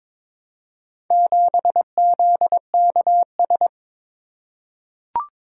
Morse code, heard as a keyed tone, about 22 wpm, 700 Hz. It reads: 7ZKS
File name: practice / 22wpm / Four Letter-Number Combinations - 22wpm